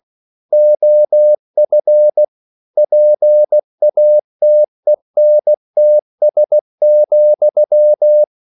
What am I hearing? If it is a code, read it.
Morse code, heard as a keyed tone, about 16 wpm, 600 Hz. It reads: OF PATENTS,
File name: CW / Practice / EnglishQuotations0416